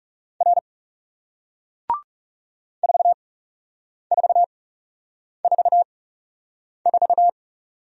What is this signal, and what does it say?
Morse code, heard as a keyed tone, about 39 wpm, spaced as for 8 wpm, 700 Hz. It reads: R 4444